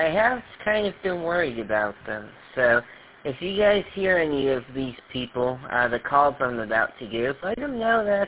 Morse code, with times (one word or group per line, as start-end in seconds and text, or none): none